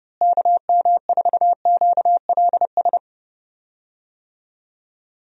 Morse code, heard as a keyed tone, about 32 wpm, 700 Hz. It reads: KM4QLS
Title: USA Call Signs - 6 Characters - Repeated 2x - 30wpm